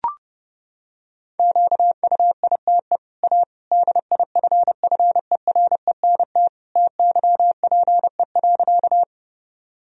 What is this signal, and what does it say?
Morse code, heard as a keyed tone, about 30 wpm, 700 Hz. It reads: QUITE A DIFFERENT TYPE.
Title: Sentences from Top 1000 Words - Review Part 3 - 30wpm